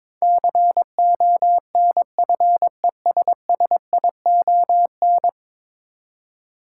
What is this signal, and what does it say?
Morse code, heard as a keyed tone, about 21 wpm, 700 Hz. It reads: CONFESSION
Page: morse-code-ninja-practice-22wpm